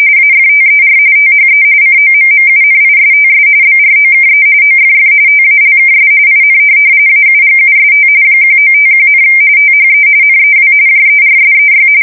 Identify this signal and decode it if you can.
RTTY, about 45.45 baud. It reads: AFTER THE COMMUNISTS BROKE OFF TALKS IN AUGUST 1951, THE NEW JERSEY
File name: fsk170 / USS-New-Jersey